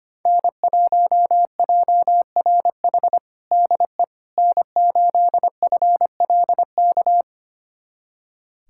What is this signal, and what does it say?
Morse code, heard as a keyed tone, about 25 wpm, 700 Hz. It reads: N1JRH DE N8FLK
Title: USA Callsign DE Callsign - 5 Characters - 25wpm